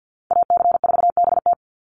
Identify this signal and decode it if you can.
Morse code, heard as a keyed tone, about 50 wpm, 700 Hz. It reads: AC4BT
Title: Call Signs Repeated 2x - 50wpm